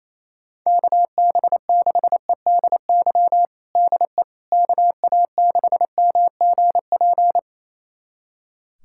Morse code, heard as a keyed tone, about 28 wpm, 700 Hz. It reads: KB6EDY DE KA6MGP
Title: USA Callsign DE Callsign - 6 Characters - 28wpm